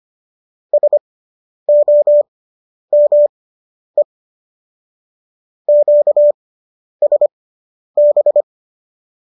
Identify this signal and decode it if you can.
Morse code, heard as a keyed tone, about 25 wpm, spaced as for 10 wpm, 590 Hz. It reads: SOME QSB